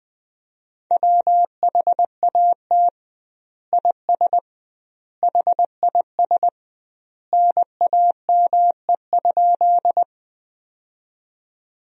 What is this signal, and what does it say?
Morse code, heard as a keyed tone, about 21 wpm, 700 Hz. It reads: WHAT IS HIS NAME?
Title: 2x Word-Spacing - Sentences from Top 200 Words - 20wpm